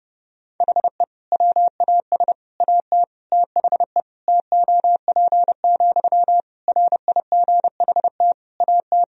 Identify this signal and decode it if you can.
Morse code, heard as a keyed tone, about 30 wpm, 700 Hz. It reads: HE WAS AT THE TOP, RIGHT AT
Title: Sentences from Top 1000 Words - Review Part 2 - 30wpm